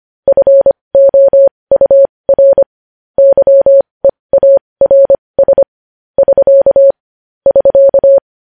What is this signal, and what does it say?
Morse code, heard as a keyed tone, about 25 wpm, spaced as for 20 wpm, 550 Hz. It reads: FOUR YEARS <SK> <SK>
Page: news-headlines-in-morse-code-at-20-wpm